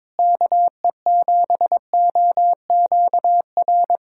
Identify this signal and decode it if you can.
Morse code, heard as a keyed tone, about 22 wpm, 700 Hz. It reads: KE7OQR
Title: USA Callsign DE Callsign - 6 Characters - 22wpm